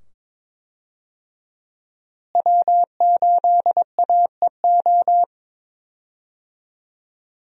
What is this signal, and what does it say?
Morse code, heard as a keyed tone, about 22 wpm, 700 Hz. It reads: W8AEO
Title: USA Call Signs - 5 Characters - Repeated 2x - 22wpm